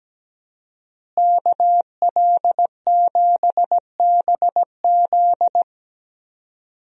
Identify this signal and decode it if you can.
Morse code, heard as a keyed tone, about 17 wpm, 700 Hz. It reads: KL7BZ
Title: DXCC-100 Call Signs - 5 Characters - Repeated 2x - 17wpm